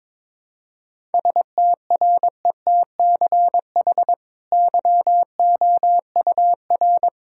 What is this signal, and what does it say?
Morse code, heard as a keyed tone, about 22 wpm, 700 Hz. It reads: STRETCH YOUR
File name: practice / 22wpm / Easy Sentences with Hints - 22wpm